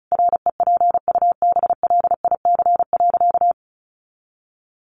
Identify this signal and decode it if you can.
Morse code, heard as a keyed tone, about 34 wpm, 700 Hz. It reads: REPUBLIC.
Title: Sentences from Top 1500 Words 35wpm